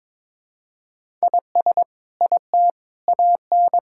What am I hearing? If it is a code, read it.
Morse code, heard as a keyed tone, about 22 wpm, 700 Hz. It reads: IS IT AN
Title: Sentences from Top 1000 Words - Review Part 6 - 22wpm